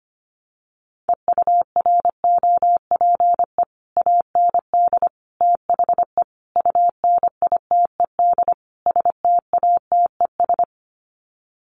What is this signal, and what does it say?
Morse code, heard as a keyed tone, about 25 wpm, 700 Hz. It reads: EUROPE AND THE UNITED STATES
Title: Sets of 5 Words Encore2 25wpm